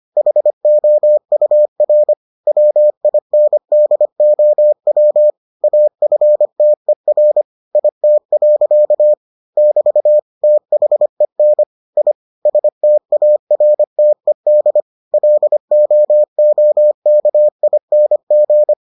Morse code, heard as a keyed tone, about 25 wpm, 600 Hz. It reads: HOUR WINDOW AFTER IT. <BT> THEN I STARTED LOOKING